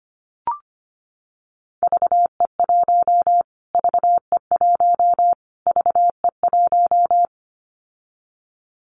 Morse code, heard as a keyed tone, about 25 wpm, 700 Hz. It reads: VE1 VE1 VE1
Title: Callsign Prefixes for Top 25 DXCC - Repeated 3x - 25wpm